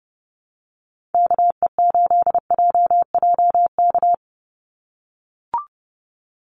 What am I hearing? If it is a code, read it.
Morse code, heard as a keyed tone, about 30 wpm, 700 Hz. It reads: KE8JJK
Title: USA Call Signs 30wpm